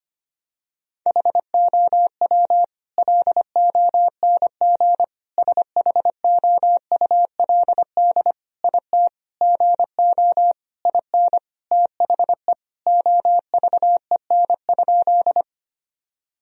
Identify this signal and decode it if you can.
Morse code, heard as a keyed tone, about 25 wpm, 700 Hz. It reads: HOW LONG SHOULD IT GO IN THE OVEN?
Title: Sentences from Top 1000 Words - Review Part 4 - 25wpm